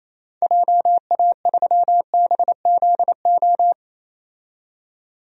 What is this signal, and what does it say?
Morse code, heard as a keyed tone, about 27 wpm, 700 Hz. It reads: JA3BZO
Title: DXCC-100 Call Signs - 6 Characters - 28wpm